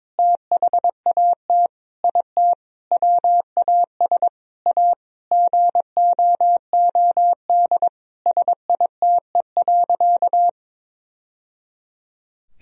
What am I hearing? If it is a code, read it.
Morse code, heard as a keyed tone, about 22 wpm, 700 Hz. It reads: THAT IT WAS A GOOD SITE.
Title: Sentences from Top 1000 Words - Review Part 4 - 22wpm